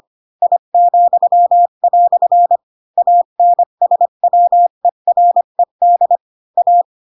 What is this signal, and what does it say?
Morse code, heard as a keyed tone, about 25 wpm, 700 Hz. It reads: I," ANSWERED A